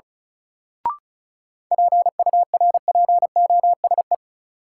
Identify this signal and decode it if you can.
Morse code, heard as a keyed tone, about 34 wpm, 700 Hz. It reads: PURPOSE
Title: Sentences from Top 800 Words - New Word Drill - 35wpm